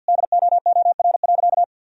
Morse code, heard as a keyed tone, about 49 wpm, 700 Hz. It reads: DOOR.
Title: Sentences from Top 1000 Words - Review Part 1 - 50wpm